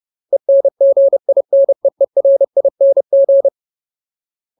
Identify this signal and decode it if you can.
Morse code, heard as a keyed tone, about 30 wpm, 550 Hz. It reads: ENGINEERING
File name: Ultimate Word List / 30WPM Ultimate